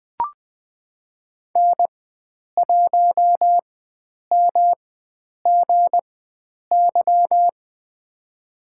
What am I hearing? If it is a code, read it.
Morse code, heard as a keyed tone, about 20 wpm, spaced as for 9 wpm, 700 Hz. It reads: N1MGY
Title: USA Call Signs - 4x Character Spacing - 20wpm